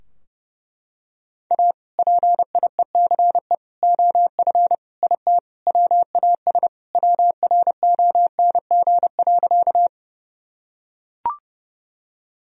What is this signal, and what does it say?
Morse code, heard as a keyed tone, about 29 wpm, 700 Hz. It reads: A PIECE OF IT WAS WRONG.
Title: Sentences from Top 1000 Words - Review Part 2 - 30wpm